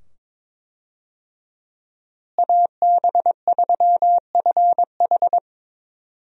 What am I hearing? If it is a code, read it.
Morse code, heard as a keyed tone, about 22 wpm, 700 Hz. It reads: AB3FH
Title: USA Call Signs - 5 Characters - Repeated 2x - 22wpm